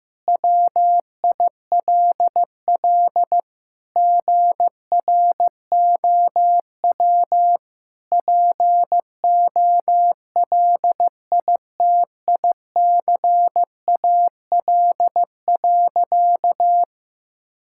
Morse code, heard as a keyed tone, about 16 wpm, 700 Hz. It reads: WILL GROW POLITICAL.
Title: Sentences from Top 1600 Words 15wpm